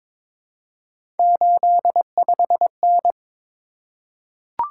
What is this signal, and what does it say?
Morse code, heard as a keyed tone, about 22 wpm, 700 Hz. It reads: 85N
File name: practice / 22wpm / Three Letter-Number Combinations - 22wpm